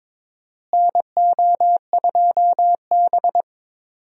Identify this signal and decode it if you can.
Morse code, heard as a keyed tone, about 22 wpm, 700 Hz. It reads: NO2B
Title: USA Call Signs - 4 Characters - Repeated 3x - 22wpm